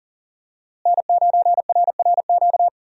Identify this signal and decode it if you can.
Morse code, heard as a keyed tone, about 40 wpm, 700 Hz. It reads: N9RRQ
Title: USA Call Signs - 5 Characters - Repeated 2x - 40wpm